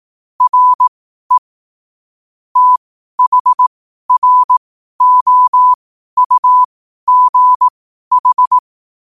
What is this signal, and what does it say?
Morse code, heard as a keyed tone, about 18 wpm, spaced as for 12 wpm, 1000 Hz. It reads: RE THROUGH
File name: 12wpm